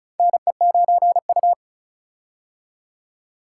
Morse code, heard as a keyed tone, about 36 wpm, 700 Hz. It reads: NE9U
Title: K1USN SST QSO Practice - 35wpm